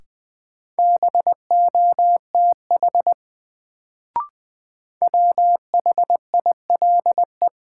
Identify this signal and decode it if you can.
Morse code, heard as a keyed tone, about 20 wpm, 700 Hz. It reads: BOTH WHILE
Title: Sentences from Top 300 Words - New Word Drill - 20wpm